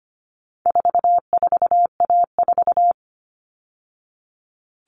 Morse code, heard as a keyed tone, about 25 wpm, 700 Hz. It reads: 44A4
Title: All Cut Numbers - 4-Digit 25wpm